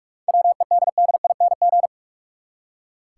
Morse code, heard as a keyed tone, about 45 wpm, 700 Hz. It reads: WEDDING
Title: Seven Letter Words - 45wpm